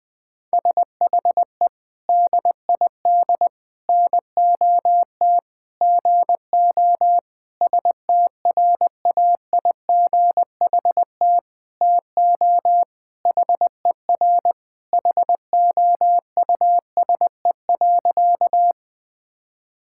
Morse code, heard as a keyed tone, about 20 wpm, 700 Hz. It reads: SHE DID NOT GO STRAIGHT TO HER HOUSE.
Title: Sentences from Top 1000 Words - Review Part 6 - 20wpm